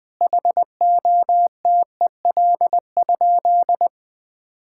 Morse code, heard as a keyed tone, about 20 wpm, 700 Hz. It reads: HOTEL?
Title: Sentences from Top 1600 Words 20wpm